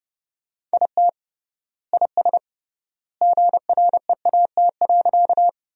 Morse code, heard as a keyed tone, about 30 wpm, 700 Hz. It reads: IT IS GREAT.
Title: 3x Word-Spacing - Sentences from Top 200 Words - 30wpm